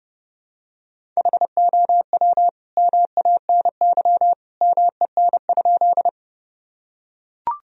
Morse code, heard as a keyed tone, about 30 wpm, 700 Hz. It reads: HOW MANY MEN?
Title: Sentences from Top 1000 Words - Review Part 1 - 30wpm